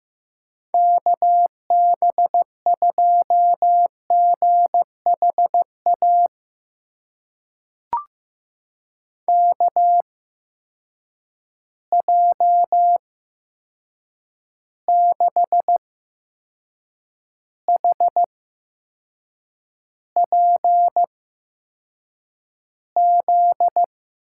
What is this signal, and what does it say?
Morse code, heard as a keyed tone, about 15 wpm, spaced as for 6 wpm, 700 Hz. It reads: KB2GHA KJ6HPZ
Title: USA Call Signs - 6 Characters - 8x Character Spacing - 15wpm